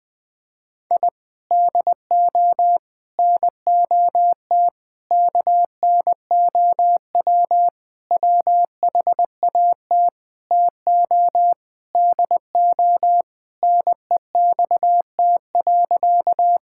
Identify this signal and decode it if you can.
Morse code, heard as a keyed tone, about 20 wpm, 700 Hz. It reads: I DO NOT KNOW WHAT TO DO NEXT.
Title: Sentences with 5-Word Phrases - 20wpm